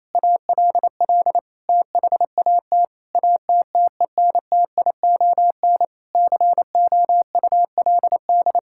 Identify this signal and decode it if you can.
Morse code, heard as a keyed tone, about 28 wpm, 700 Hz. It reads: ALL THAT ATTENTION COULD
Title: Sentences from Top 1000 Words - Review Part 6 - 28wpm